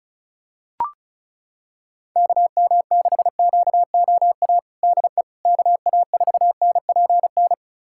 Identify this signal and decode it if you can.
Morse code, heard as a keyed tone, about 35 wpm, 700 Hz. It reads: KM6QOA DE KA4NPN
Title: USA Callsign DE Callsign - 6 Characters - 35wpm